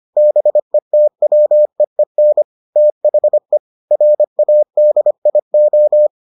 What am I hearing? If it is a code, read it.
Morse code, heard as a keyed tone, about 25 wpm, 600 Hz. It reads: BETWEEN THE RADIO